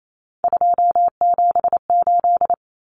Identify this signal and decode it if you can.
Morse code, heard as a keyed tone, about 28 wpm, 700 Hz. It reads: 278
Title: Common Cut Numbers - 3-Digit 28wpm